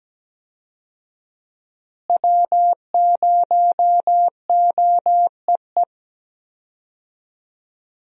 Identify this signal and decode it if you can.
Morse code, heard as a keyed tone, about 17 wpm, 700 Hz. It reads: W0OEE
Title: USA Call Signs - 5 Characters - Repeated 3x - 17wpm